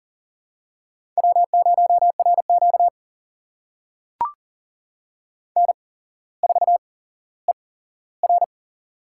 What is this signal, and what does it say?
Morse code, heard as a keyed tone, about 40 wpm, spaced as for 17 wpm, 700 Hz. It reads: W0RQ N4ER
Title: USA Call Signs - 4 Characters - 8x Character Spacing - 40wpm